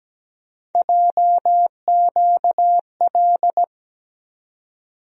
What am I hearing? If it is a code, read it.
Morse code, heard as a keyed tone, about 17 wpm, 700 Hz. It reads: JQL
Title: USA Call Sign Postfixes - 3 Letters 17wpm